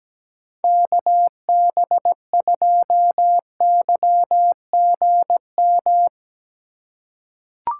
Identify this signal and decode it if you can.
Morse code, heard as a keyed tone, about 17 wpm, 700 Hz. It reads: KB2YGM
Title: USA Call Signs - 6 Characters - 4x Character Spacing - 17wpm